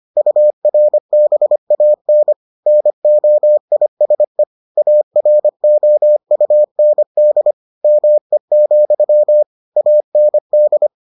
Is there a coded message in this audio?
Yes: Morse code from 0.2 to 10.9 s, heard as a keyed tone, about 25 wpm, 600 Hz. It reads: URBAN NOISE AROUND ME, AND